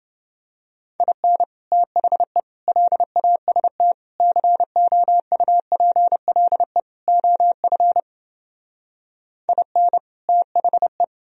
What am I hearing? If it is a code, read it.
Morse code, heard as a keyed tone, about 29 wpm, 700 Hz. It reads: IN THE LAST COUPLE OF IN THE